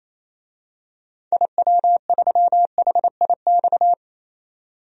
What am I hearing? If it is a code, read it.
Morse code, heard as a keyed tone, about 28 wpm, 700 Hz. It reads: IW3HIX
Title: DXCC-100 Call Signs - 6 Characters - 28wpm